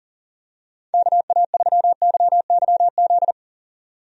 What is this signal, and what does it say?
Morse code, heard as a keyed tone, about 40 wpm, 700 Hz. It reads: KA3YYZ
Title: USA Call Signs - 6 Characters - 40wpm